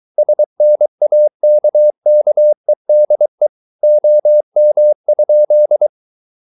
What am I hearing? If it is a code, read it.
Morse code, heard as a keyed tone, about 23 wpm, 600 Hz. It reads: SNAKKEDE OM?
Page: Ferie 23 wpm